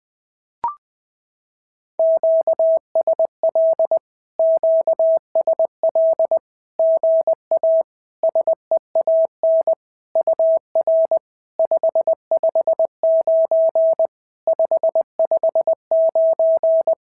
Morse code, heard as a keyed tone, about 20 wpm, 650 Hz. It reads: QSL QSL GA SEAN UR 559 559
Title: SOTA QSO Practice 20wpm